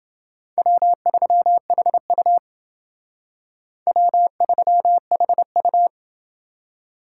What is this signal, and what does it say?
Morse code, heard as a keyed tone, about 28 wpm, 700 Hz. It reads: W3HU W3HU